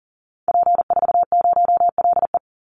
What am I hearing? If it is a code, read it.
Morse code, heard as a keyed tone, about 40 wpm, 700 Hz. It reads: P40LE
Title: Call Signs Repeated 2x - 40wpm